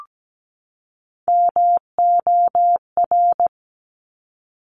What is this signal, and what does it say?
Morse code, heard as a keyed tone, about 17 wpm, 700 Hz. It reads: MOR
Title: Top Three Letter Combinations in Words 17wpm